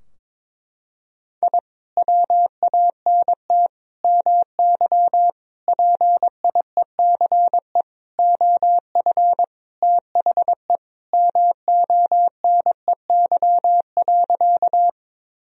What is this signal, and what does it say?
Morse code, heard as a keyed tone, about 22 wpm, 700 Hz. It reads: I WANT MY PIECE OF THE MONEY.
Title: Sentences from Top 1000 Words - Review Part 2 - 22wpm